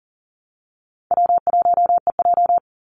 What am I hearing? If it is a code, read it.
Morse code, heard as a keyed tone, about 40 wpm, 700 Hz. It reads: W1EJ
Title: NA SOTA Activator Callsigns 40wpm